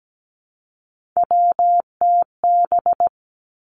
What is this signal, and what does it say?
Morse code, heard as a keyed tone, about 17 wpm, 700 Hz. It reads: WTB
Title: Three Challenging Letter-Number Combinations 17wpm